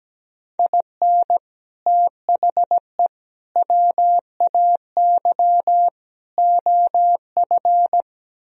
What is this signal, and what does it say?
Morse code, heard as a keyed tone, about 17 wpm, 700 Hz. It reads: IN THE WAY OF
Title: Speed-Racing - Sets of 4 Words 17wpm